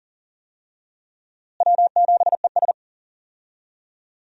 Morse code, heard as a keyed tone, about 37 wpm, 700 Hz. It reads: W7ES
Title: USA Call Signs - 4 Characters - Repeated 2x - 40wpm